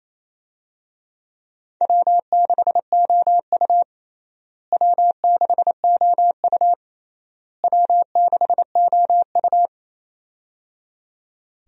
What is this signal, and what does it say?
Morse code, heard as a keyed tone, about 28 wpm, 700 Hz. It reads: W6OU W6OU W6OU
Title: USA Call Signs - 4 Characters - Repeated 3x - 28wpm